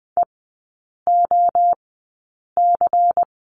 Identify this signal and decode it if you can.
Morse code, heard as a keyed tone, about 20 wpm, spaced as for 8 wpm, 700 Hz. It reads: EOC
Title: 093 Lesson 16e - Letter c - Three Characters 20wpm